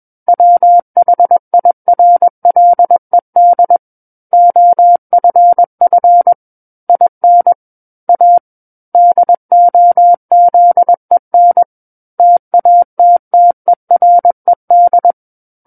Morse code, heard as a keyed tone, about 21 wpm, 700 Hz. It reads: WHIRLED OFF IN A DOZEN TATTERED
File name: Book One / 12-Chapter 12 - 21 WPM